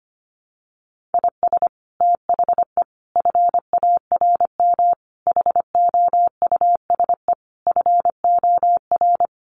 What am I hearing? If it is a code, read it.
Morse code, heard as a keyed tone, about 25 wpm, 700 Hz. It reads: IS THE FARM HOUSE FOR
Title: Sentences from Top 1400 Words 25wpm